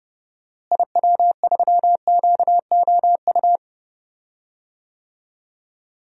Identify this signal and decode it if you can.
Morse code, heard as a keyed tone, about 31 wpm, 700 Hz. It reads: IW3QOU
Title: DXCC-100 Call Signs - 6 Characters - Repeated 3x - 30wpm